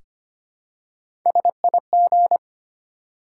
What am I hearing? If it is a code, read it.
Morse code, heard as a keyed tone, about 25 wpm, 700 Hz. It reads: SIG
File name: Top 90 QSO Elements - 4x Character Spacing - 25wpm